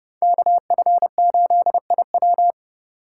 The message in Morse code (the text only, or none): KF8IW